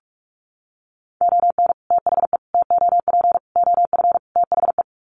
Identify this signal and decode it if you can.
Morse code, heard as a keyed tone, about 44 wpm, 700 Hz. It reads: ON THE TOP OF THE
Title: Sets of 5 Words Encore 45wpm